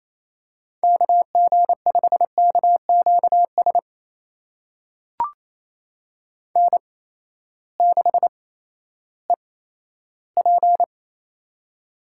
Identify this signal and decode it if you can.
Morse code, heard as a keyed tone, about 28 wpm, spaced as for 14 wpm, 700 Hz. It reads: KG5KQS N6EP
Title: USA Call Signs - 8x Character Spacing - 28wpm